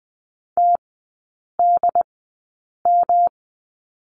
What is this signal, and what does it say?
Morse code, heard as a keyed tone, about 20 wpm, spaced as for 8 wpm, 700 Hz. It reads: TDM
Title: 107 Lesson 18e - Letter m - Three Characters 20wpm